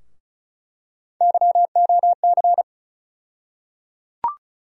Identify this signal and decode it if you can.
Morse code, heard as a keyed tone, about 36 wpm, 700 Hz. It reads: YOC
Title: USA Call Sign Postfixes - 3 Letters 35wpm